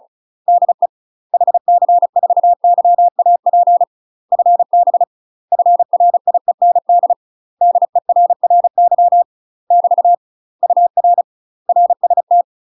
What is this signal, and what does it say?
Morse code, heard as a keyed tone, about 36 wpm, 700 Hz. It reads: DE HC4YAP FB FRIEND DERRY <BT> UR RST
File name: UK MASTER QSOs TRAINING / 35WPM / UK MASTER QSOs 0036